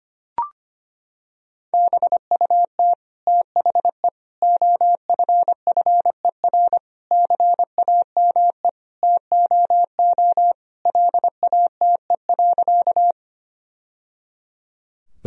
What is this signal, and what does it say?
Morse code, heard as a keyed tone, about 25 wpm, 700 Hz. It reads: BUT THE OFFER CAME TOO LATE.
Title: Sentences from Top 1000 Words - Review Part 4 - 25wpm